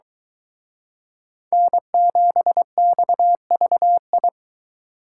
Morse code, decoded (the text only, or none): N7XVI